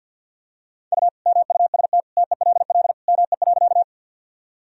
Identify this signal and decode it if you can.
Morse code, heard as a keyed tone, about 49 wpm, 700 Hz. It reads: U MUST TELL ME.
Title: Sentences from Top 200 Words and Abbreviations 50wpm